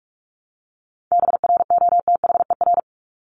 Morse code, heard as a keyed tone, about 44 wpm, 700 Hz. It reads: BROTHER
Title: Top 1751-2000 Words 45wpm